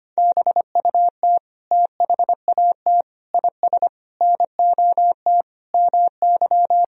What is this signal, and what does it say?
Morse code, heard as a keyed tone, about 25 wpm, 700 Hz. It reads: BUT THAT IS NOT MY
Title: Sentences from Top 1000 Words - Review Part 4 - 25wpm